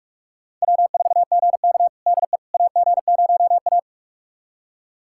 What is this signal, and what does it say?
Morse code, heard as a keyed tone, about 42 wpm, 700 Hz. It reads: W4GK DE AG0A
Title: USA Callsign DE Callsign - 4 Characters - 45wpm